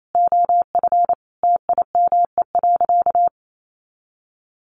Morse code, heard as a keyed tone, about 28 wpm, 700 Hz. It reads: OF TIME.
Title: Sentences from Top 1500 Words 28wpm